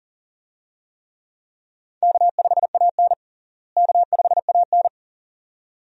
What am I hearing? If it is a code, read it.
Morse code, heard as a keyed tone, about 40 wpm, 700 Hz. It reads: K5AN K5AN
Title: USA Call Signs - 4 Characters - Repeated 2x - 40wpm